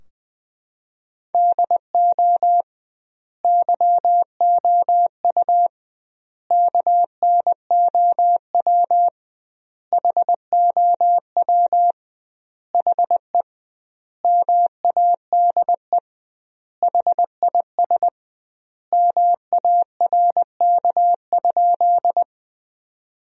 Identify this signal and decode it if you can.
Morse code, heard as a keyed tone, about 20 wpm, 700 Hz. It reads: DO YOU KNOW HOW HE MADE HIS MARK?
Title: 2x Word-Spacing - Sentences from Top 300 Words - 20wpm